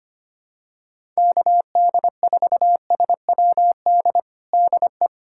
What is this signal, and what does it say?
Morse code, heard as a keyed tone, about 25 wpm, 700 Hz. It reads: KD4SWD DE
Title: USA Callsign DE Callsign - 6 Characters - 25wpm